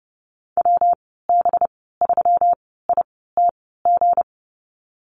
Morse code, heard as a keyed tone, about 30 wpm, spaced as for 17 wpm, 700 Hz. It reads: WB3ITG
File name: Lesson 37g - Letter x - Call Signs 30wpm